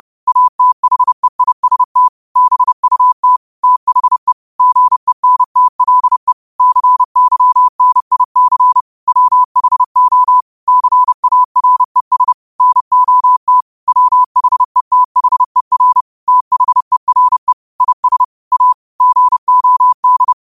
Morse code, heard as a keyed tone, about 30 wpm, 1000 Hz. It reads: ATHEIST BUT THE GENTLE CYNIC WHO CARES NOT WHETHER THERE IS A GOD
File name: qotdfiles / 30wpm